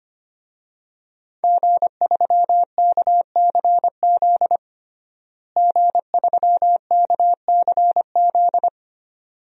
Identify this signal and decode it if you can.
Morse code, heard as a keyed tone, about 25 wpm, 700 Hz. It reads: G3KCZ G3KCZ
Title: DXCC-100 Call Signs - 5 Characters - Repeated 2x - 25wpm